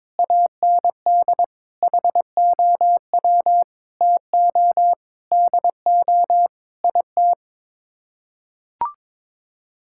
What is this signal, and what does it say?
Morse code, heard as a keyed tone, about 22 wpm, 700 Hz. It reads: AND HOW TO DO IT
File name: practice / 22wpm / Sets of 8 Words - 22wpm